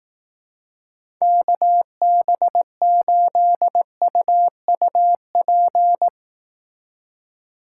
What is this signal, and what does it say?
Morse code, heard as a keyed tone, about 18 wpm, 700 Hz. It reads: KB8UUP
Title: Speed Racing - USA 2x3 Call Signs 15wpm